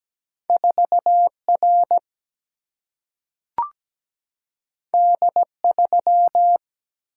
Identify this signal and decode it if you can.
Morse code, heard as a keyed tone, about 17 wpm, 700 Hz. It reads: 4 R D 3
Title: Single-Letter Single-Number 17wpm